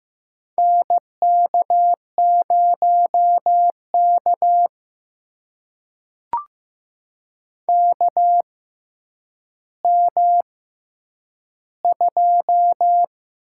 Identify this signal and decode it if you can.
Morse code, heard as a keyed tone, about 15 wpm, spaced as for 9 wpm, 700 Hz. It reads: NK0K KM2
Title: USA Call Signs - 4 Characters - 6x Character Spacing - 15wpm